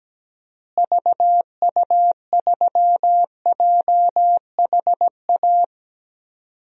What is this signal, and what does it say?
Morse code, heard as a keyed tone, about 17 wpm, 700 Hz. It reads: VU3JHA
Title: DXCC-100 Call Signs - 6 Characters - Repeated 2x - 17wpm